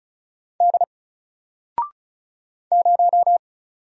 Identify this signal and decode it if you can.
Morse code, heard as a keyed tone, about 36 wpm, 700 Hz. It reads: D 0
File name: Single-Letter-Number-35wpm